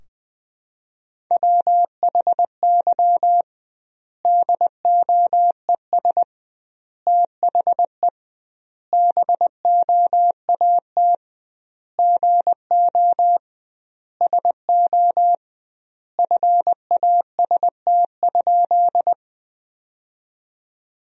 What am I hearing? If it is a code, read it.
Morse code, heard as a keyed tone, about 20 wpm, 700 Hz. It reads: WHY DOES THE BOAT GO SO FAST?
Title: 2x Word-Spacing - Sentences from Top 500 Words - 20wpm